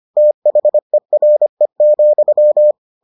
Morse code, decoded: THERE,

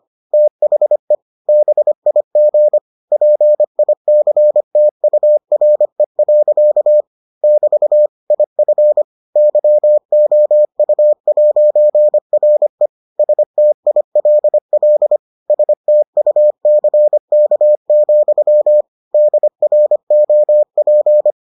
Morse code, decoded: THE BIG PICTURE. <BT> IF YOU'RE STILL STUCK, DROP

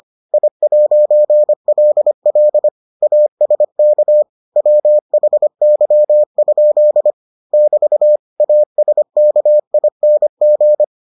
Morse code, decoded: I'LL ASK WHY? <BT> ASKING

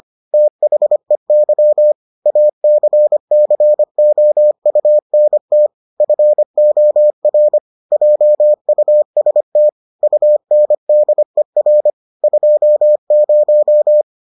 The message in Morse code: THEY ACCOUNT FOR JUST UNDER 20

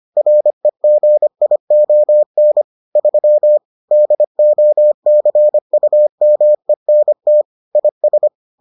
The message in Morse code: REGION 3 DOCUMENT IS

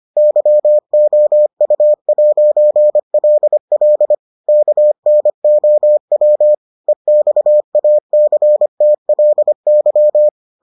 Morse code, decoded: YOU'LL KNOW EXACTLY

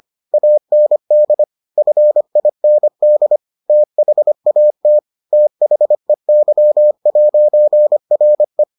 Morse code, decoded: AND FIND THAT THEY'RE

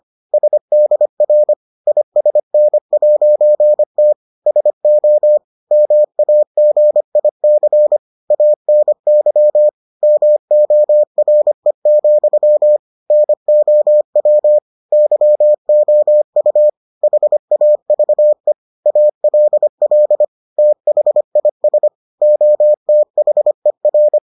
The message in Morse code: SDR ISN'T SO MAGIC ANY MORE, NOW YOU HAVE ALL THIS OTHER

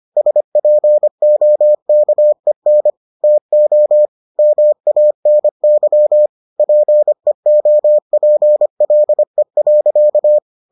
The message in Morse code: SPOKEN TO MANY PEOPLE.

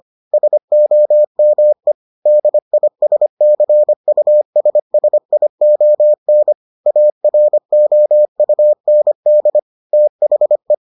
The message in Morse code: SOME DISCUSSION AROUND THE

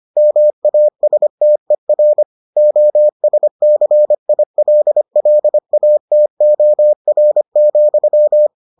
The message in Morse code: MASTER OSCILLATOR,

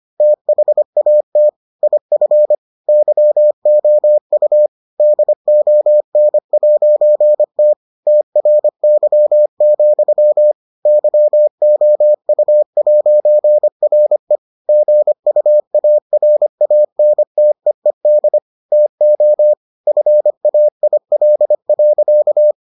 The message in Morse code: THAT IF YOU DON'T TRY, YOU'RE GUARANTEED TO FAIL.

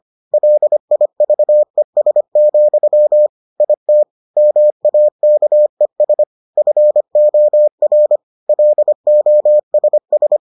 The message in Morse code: LIVES, IT MAKES FOR LOSS

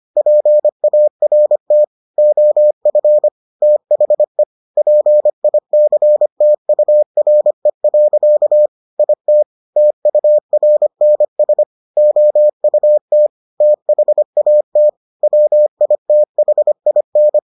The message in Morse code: PART OF THE PICTURE. IT TURNS OUT THAT WITHIN

